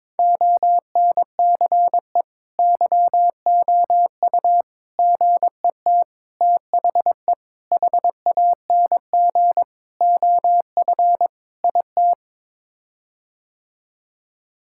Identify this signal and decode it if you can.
Morse code, heard as a keyed tone, about 22 wpm, 700 Hz. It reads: ONCE YOU GET THE HANG OF IT
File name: practice / 22wpm / Sets of 7 Words - 22wpm